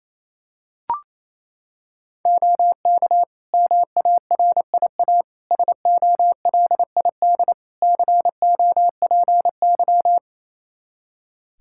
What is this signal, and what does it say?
Morse code, heard as a keyed tone, about 28 wpm, 700 Hz. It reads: OK MARIA SOLID COPY
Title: Practice QSO - Exchange 3 - More Info - 28wpm